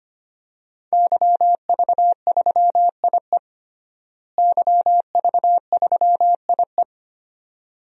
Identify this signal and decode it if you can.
Morse code, heard as a keyed tone, about 25 wpm, 700 Hz. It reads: YV3IE YV3IE
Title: DXCC-100 Call Signs - 5 Characters - Repeated 2x - 25wpm